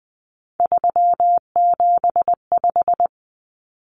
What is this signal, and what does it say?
Morse code, heard as a keyed tone, about 20 wpm, 700 Hz. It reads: 375
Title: Common Cut Numbers - 3-Digit 20wpm